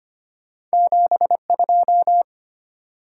Morse code, heard as a keyed tone, about 33 wpm, 700 Hz. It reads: 7 2